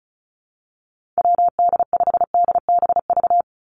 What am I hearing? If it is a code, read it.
Morse code, heard as a keyed tone, about 34 wpm, 700 Hz. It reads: WD5DBV
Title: Call Signs - 35wpm